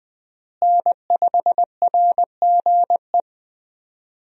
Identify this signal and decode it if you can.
Morse code, heard as a keyed tone, about 20 wpm, 700 Hz. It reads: N5RGE
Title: USA Call Signs - 5 Characters - Repeated 2x - 20wpm